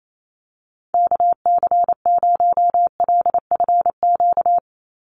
Morse code, heard as a keyed tone, about 28 wpm, 700 Hz. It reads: KC0LFQ
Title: USA Call Signs 28wpm